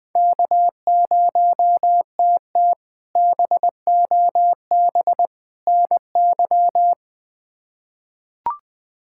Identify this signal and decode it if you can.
Morse code, heard as a keyed tone, about 21 wpm, 700 Hz. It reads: K0TT BOB NY
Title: K1USN SST QSO Practice - 20wpm